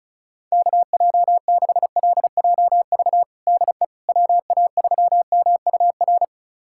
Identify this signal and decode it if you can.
Morse code, heard as a keyed tone, about 36 wpm, 700 Hz. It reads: KJ6LJV DE WA3MUR
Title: USA Callsign DE Callsign - 6 Characters - 35wpm